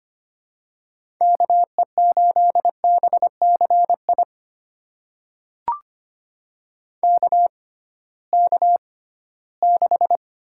Morse code, heard as a keyed tone, about 25 wpm, spaced as for 16 wpm, 700 Hz. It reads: KE8BCI KK6
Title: USA Call Signs - 6 Characters - 6x Character Spacing - 25wpm